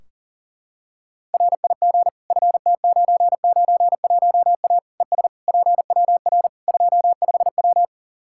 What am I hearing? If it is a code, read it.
Morse code, heard as a keyed tone, about 40 wpm, 700 Hz. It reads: RIG FT991A ES PWR 25W
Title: Practice QSO - Exchange 2 - Info - 40wpm